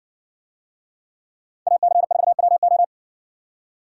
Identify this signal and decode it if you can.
Morse code, heard as a keyed tone, about 60 wpm, 700 Hz. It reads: AC4WO